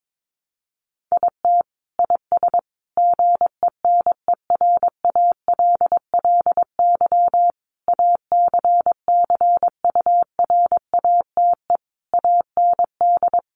Morse code, heard as a keyed tone, about 22 wpm, 700 Hz. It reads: IT IS GENERALLY ACCURATE AND
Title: Sentences from Top 1200 Words 22wpm